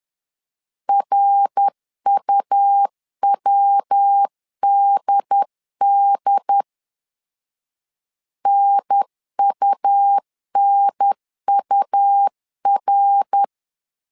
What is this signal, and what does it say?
Morse code, heard as a keyed tone, about 10 wpm, 800 Hz. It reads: RUWDD NUNUR
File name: LEZIONI / 3 MWRO / 3 MWR 10wpm 1952